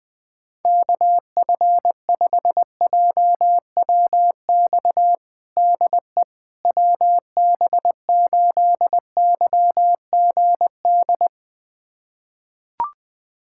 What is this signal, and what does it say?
Morse code, heard as a keyed tone, about 20 wpm, 700 Hz. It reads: KF5JWX DE WB8YGD
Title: USA Callsign DE Callsign 20wpm